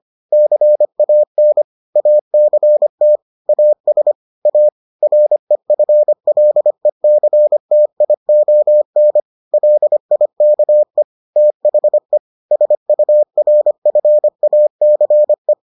Morse code, heard as a keyed tone, about 25 wpm, 600 Hz. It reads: CAN ACT AS A REFLECTION LIKE THE SURFACE